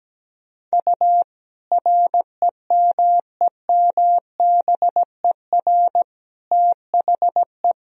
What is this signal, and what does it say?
Morse code, heard as a keyed tone, about 17 wpm, 700 Hz. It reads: U REMEMBER THE
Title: Sentences from Top 400 Words and Abbreviations 17wpm